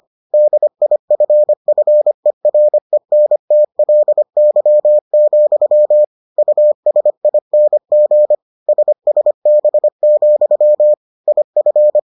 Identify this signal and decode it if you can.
Morse code, heard as a keyed tone, about 25 wpm, 600 Hz. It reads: DIFFERENTLY, USING SSB, IF